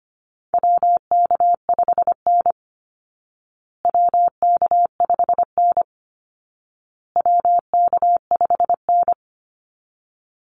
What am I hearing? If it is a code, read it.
Morse code, heard as a keyed tone, about 25 wpm, 700 Hz. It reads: WK5N WK5N WK5N
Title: Call Signs Repeated 2x - 25wpm